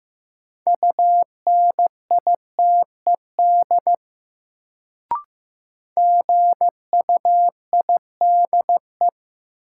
Morse code, heard as a keyed tone, about 15 wpm, 700 Hz. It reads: UNITED GUIDE
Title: Sentences from Top 600 Words - New Word Drill - 15wpm